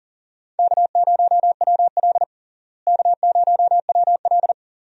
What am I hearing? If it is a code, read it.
Morse code, heard as a keyed tone, about 37 wpm, 700 Hz. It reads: K0WL K0WL
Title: USA Call Signs - 4 Characters - Repeated 3x - 40wpm